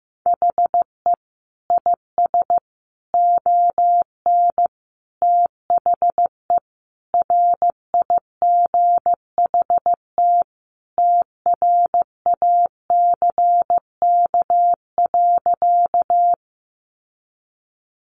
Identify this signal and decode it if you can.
Morse code, heard as a keyed tone, about 15 wpm, 700 Hz. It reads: HE IS ON THE RIGHT TRACK.
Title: Sentences from Top 1100 Words 15wpm